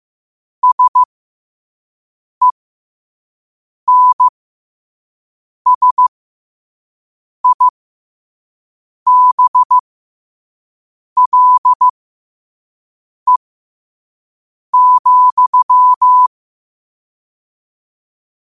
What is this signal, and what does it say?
Morse code, heard as a keyed tone, about 15 wpm, spaced as for 5 wpm, 1000 Hz. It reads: SENSIBLE,
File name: qotdfiles / 5wpm